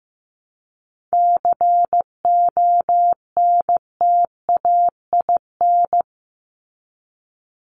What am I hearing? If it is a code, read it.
Morse code, heard as a keyed tone, about 15 wpm, 700 Hz. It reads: CONTAIN
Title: ING Suffix - Top 100 Words with Base Words 15wpm